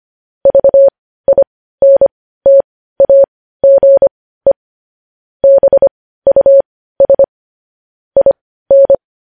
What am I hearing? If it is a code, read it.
Morse code, heard as a keyed tone, about 25 wpm, spaced as for 15 wpm, 550 Hz. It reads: VINTAGE BUS IN
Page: News Headlines in Morse Code at 15 WPM